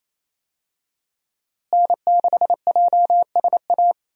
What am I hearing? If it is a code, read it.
Morse code, heard as a keyed tone, about 27 wpm, 700 Hz. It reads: N6JSA
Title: USA Call Signs - 5 Characters - Repeated 3x - 28wpm